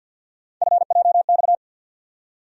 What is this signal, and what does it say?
Morse code, heard as a keyed tone, about 52 wpm, 700 Hz. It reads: FJX
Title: USA Call Sign Postfixes - 3 Letters 50wpm